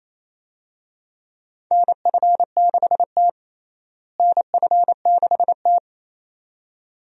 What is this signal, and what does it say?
Morse code, heard as a keyed tone, about 28 wpm, 700 Hz. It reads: NF6T NF6T
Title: USA Call Signs - 4 Characters - Repeated 2x - 28wpm